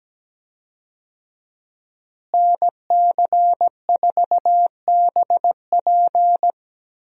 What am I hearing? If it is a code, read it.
Morse code, heard as a keyed tone, about 17 wpm, 700 Hz. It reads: NC4BP
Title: USA Call Signs - 5 Characters - Repeated 2x - 17wpm